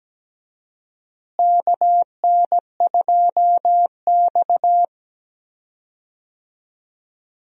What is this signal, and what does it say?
Morse code, heard as a keyed tone, about 17 wpm, 700 Hz. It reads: KN2X
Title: USA Call Signs - 4 Characters - Repeated 3x - 17wpm